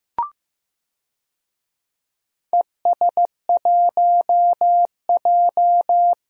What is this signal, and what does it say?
Morse code, heard as a keyed tone, about 16 wpm, 700 Hz. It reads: ES1J
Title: DXCC-100 Call Signs - 4 Characters - Repeated 2x - 15wpm